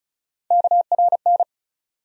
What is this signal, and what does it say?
Morse code, heard as a keyed tone, about 36 wpm, 700 Hz. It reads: KRN
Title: USA Call Sign Postfixes - 3 Letters 35wpm